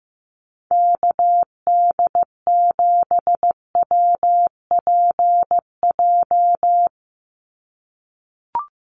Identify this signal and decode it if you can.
Morse code, heard as a keyed tone, about 15 wpm, 700 Hz. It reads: KD7WPJ
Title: NA SOTA Activator Callsigns 15wpm